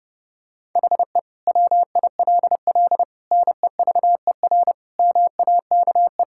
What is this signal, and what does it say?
Morse code, heard as a keyed tone, about 32 wpm, 700 Hz. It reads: HE WILL NEVER MAKE
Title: Sentences from Top 1600 Words 30wpm